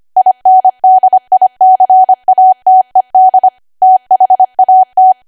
Morse code, heard as a keyed tone, about 25 wpm, 750 Hz. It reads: INDICATED THAT
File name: Archive / 25 WPM / 25WPM